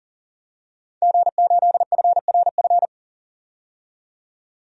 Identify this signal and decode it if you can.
Morse code, heard as a keyed tone, about 40 wpm, 700 Hz. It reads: G8FRF
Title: DXCC-100 Call Signs - 5 Characters - 40wpm